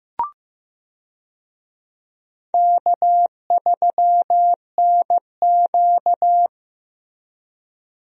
Morse code, heard as a keyed tone, about 15 wpm, 700 Hz. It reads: K3NQ
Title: USA Call Signs - 4 Characters - Repeated 2x - 15wpm